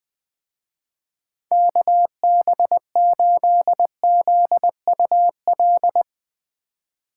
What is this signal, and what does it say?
Morse code, heard as a keyed tone, about 21 wpm, 700 Hz. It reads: KB8ZUL
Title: Speed Racing - USA 2x3 Call Signs 17wpm